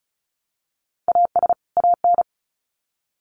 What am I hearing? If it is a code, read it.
Morse code, heard as a keyed tone, about 34 wpm, 700 Hz. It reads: AS AN
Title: Sets of 2 Words 35wpm